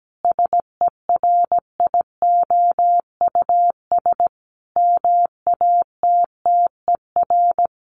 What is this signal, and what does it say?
Morse code, heard as a keyed tone, about 17 wpm, 700 Hz. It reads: SERIOUS MATTER
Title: Sentences from Top 1100 Words 17wpm